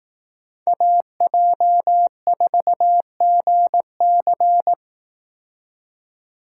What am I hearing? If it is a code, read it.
Morse code, heard as a keyed tone, about 18 wpm, 700 Hz. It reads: AJ4GC
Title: Speed Racing - USA 2x2 Call Signs 15wpm